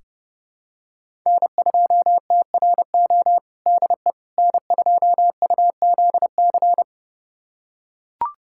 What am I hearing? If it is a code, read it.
Morse code, heard as a keyed tone, about 30 wpm, 700 Hz. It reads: N2TRO DE N2UZC
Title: USA Callsign DE Callsign - 5 Characters - 30wpm